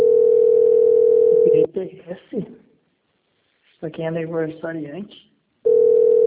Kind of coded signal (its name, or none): telephone call progress tone